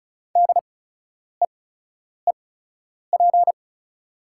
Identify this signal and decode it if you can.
Morse code, heard as a keyed tone, about 34 wpm, spaced as for 10 wpm, 690 Hz. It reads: DEEP
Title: 8x Character Spacing - Top 500 Words - 35wpm